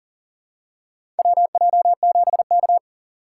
Morse code, heard as a keyed tone, about 40 wpm, 700 Hz. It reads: WJ7K
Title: USA Call Signs - 4 Characters - 2x Character Spacing - 40wpm